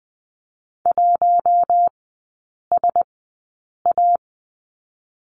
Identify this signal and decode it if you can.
Morse code, heard as a keyed tone, about 20 wpm, spaced as for 8 wpm, 700 Hz. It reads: 1SA
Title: 074 Lesson 13e - Letter L - Three Characters 20wpm